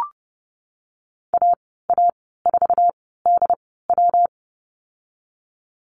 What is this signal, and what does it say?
Morse code, heard as a keyed tone, about 30 wpm, 700 Hz. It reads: A A 4 D W
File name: Lesson 19g - Letter w - Call Signs 30wpm